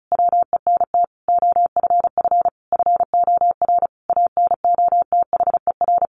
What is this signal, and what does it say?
Morse code, heard as a keyed tone, about 34 wpm, 700 Hz. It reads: WENT OFF FOR ANOTHER
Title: Sentences from Top 1500 Words 35wpm